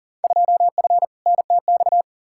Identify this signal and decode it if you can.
Morse code, heard as a keyed tone, about 40 wpm, 700 Hz. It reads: 2F NTX
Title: Field Day Exchanges - 40wpm